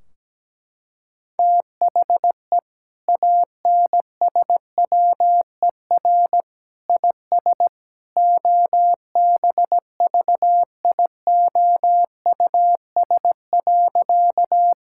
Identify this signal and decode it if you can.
Morse code, heard as a keyed tone, about 17 wpm, 700 Hz. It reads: THE ANSWER IS OBVIOUS.
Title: Sentences from Top 1600 Words 17wpm